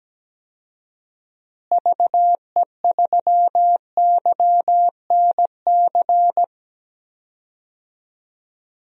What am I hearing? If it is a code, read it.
Morse code, heard as a keyed tone, about 17 wpm, 700 Hz. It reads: VE3YNC
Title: DXCC-100 Call Signs - 6 Characters - Repeated 2x - 17wpm